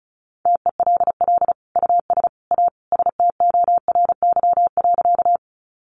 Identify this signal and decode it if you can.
Morse code, heard as a keyed tone, about 34 wpm, 700 Hz. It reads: TELL US A STORY.
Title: Sentences from Top 300 Words 35wpm